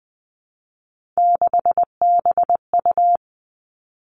Morse code, heard as a keyed tone, about 19 wpm, 700 Hz. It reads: 6BU
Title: Three Challenging Letter-Number Combinations 20wpm